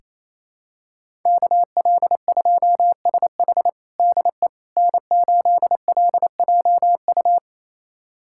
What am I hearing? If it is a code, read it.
Morse code, heard as a keyed tone, about 28 wpm, 700 Hz. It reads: KL2SH DE N8LJU
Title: USA Callsign DE Callsign 28wpm